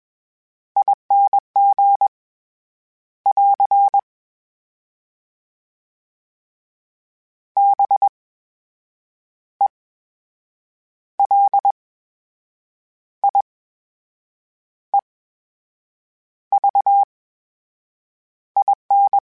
Morse code, heard as a keyed tone, about 21 wpm, spaced as for 7 wpm, 800 Hz. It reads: ING <AR> BELIEVIN
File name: Special INGs-mp3s / Special INGs-mp3s / VerySlowToFaster Ing @20